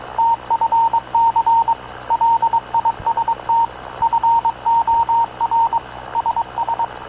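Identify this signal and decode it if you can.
Morse code, heard as a keyed tone, about 22 wpm, 920 Hz, with radio noise. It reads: TFC LIST FOR SS
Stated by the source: Havana Radio / CLA sending Spanish press (PX) in Morse, recorded by Radio Officer Paul Dumesnil onboard CNAV QUEST/CZDO.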